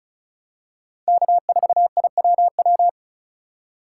Morse code, heard as a keyed tone, about 36 wpm, 700 Hz. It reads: K4IWW
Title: USA Call Signs - 5 Characters - 2x Character Spacing - 35wpm